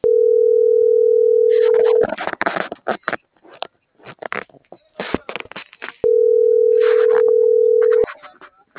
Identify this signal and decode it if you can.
Ringback tone (North America), 440 + 480 Hz.